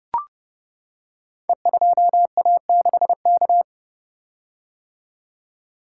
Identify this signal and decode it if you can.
Morse code, heard as a keyed tone, about 30 wpm, 700 Hz. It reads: E2A6K
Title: Code Groups - 5 Letter:Numbers - 30wpm